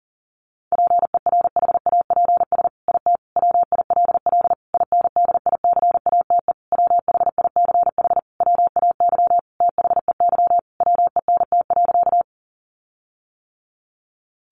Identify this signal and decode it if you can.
Morse code, heard as a keyed tone, about 40 wpm, 700 Hz. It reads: PERHAPS IT WILL INDICATE WHICH WAY THEY WENT.
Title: Sentences from Top 1300 Words 40wpm